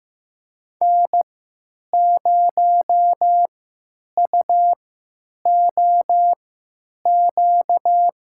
Morse code, heard as a keyed tone, about 15 wpm, spaced as for 9 wpm, 700 Hz. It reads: N0UOQ
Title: USA Call Signs - 5 Characters - 3x Character Spacing - 15wpm